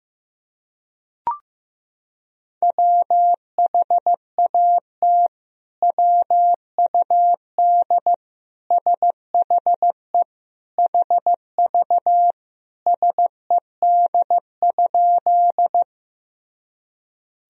Morse code, heard as a keyed tone, about 16 wpm, 700 Hz. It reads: WHAT WUD SHE HV SED?
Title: Sentences from Top 100 Words and Abbreviations 15wpm